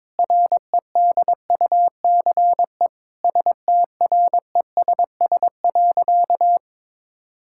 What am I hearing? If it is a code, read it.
Morse code, heard as a keyed tone, about 22 wpm, 700 Hz. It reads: REDUCE STRESS.
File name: Difficult Sentences with Hints - 22wpm